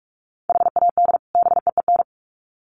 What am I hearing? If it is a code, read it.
Morse code, heard as a keyed tone, about 44 wpm, 700 Hz. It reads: HAD BEEN
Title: Sets of 2 Words 45wpm